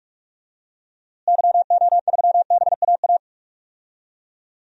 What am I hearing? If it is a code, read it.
Morse code, heard as a keyed tone, about 46 wpm, 700 Hz. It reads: YO3BAA
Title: DXCC-100 Call Signs - 6 Characters - Repeated 3x - 45wpm